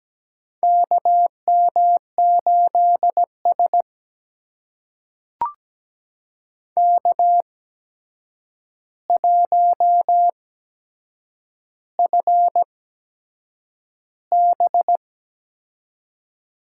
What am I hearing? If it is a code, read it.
Morse code, heard as a keyed tone, about 17 wpm, spaced as for 7 wpm, 700 Hz. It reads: KM8S K1FB
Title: USA Call Signs - 4 Characters - 8x Character Spacing - 17wpm